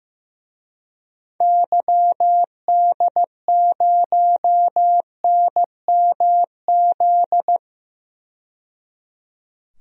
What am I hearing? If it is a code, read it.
Morse code, heard as a keyed tone, about 15 wpm, 700 Hz. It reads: YD0NMZ
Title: DXCC-100 Call Signs - 6 Characters - 15wpm